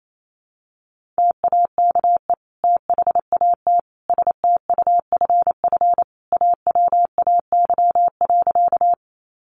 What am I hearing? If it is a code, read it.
Morse code, heard as a keyed tone, about 28 wpm, 700 Hz. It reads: TAKE THAT STUFF AWAY.
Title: Sentences from Top 1400 Words 28wpm